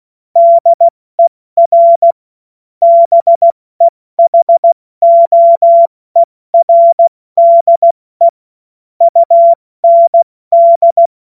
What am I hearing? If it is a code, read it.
Morse code, heard as a keyed tone, about 16 wpm, 680 Hz. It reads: DER BEHOERDE UND
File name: Witz-02-September-2025-70-F80